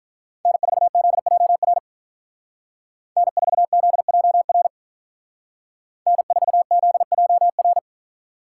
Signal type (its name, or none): Morse code